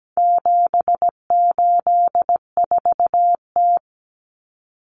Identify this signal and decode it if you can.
Morse code, heard as a keyed tone, about 17 wpm, 700 Hz. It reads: 784T
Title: Common Cut Numbers - 4-Digit 17wpm